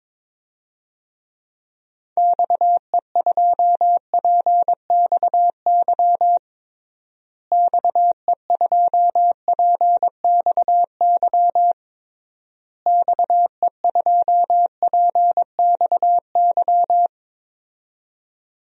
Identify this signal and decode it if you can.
Morse code, heard as a keyed tone, about 22 wpm, 700 Hz. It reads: XE2PXY XE2PXY XE2PXY
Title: DXCC-100 Call Signs - 6 Characters - Repeated 3x - 22wpm